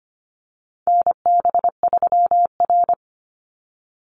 Morse code, heard as a keyed tone, about 25 wpm, 700 Hz. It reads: NB3R
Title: Call Signs Repeated 2x - 25wpm